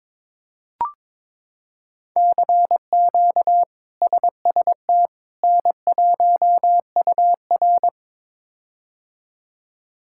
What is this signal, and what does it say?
Morse code, heard as a keyed tone, about 22 wpm, 700 Hz. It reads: CQ SST N1UR
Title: K1USN SST QSO Practice - 22wpm